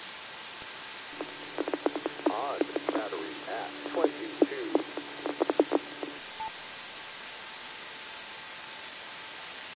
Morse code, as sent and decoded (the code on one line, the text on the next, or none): none